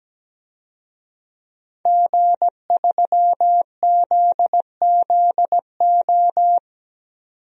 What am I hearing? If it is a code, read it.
Morse code, heard as a keyed tone, about 17 wpm, 700 Hz. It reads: G3ZZO
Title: DXCC-100 Call Signs - 5 Characters - 17wpm